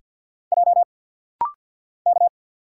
Morse code, heard as a keyed tone, about 48 wpm, 700 Hz. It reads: J K
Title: Single Letter-Number - Rapid-Fire 50wpm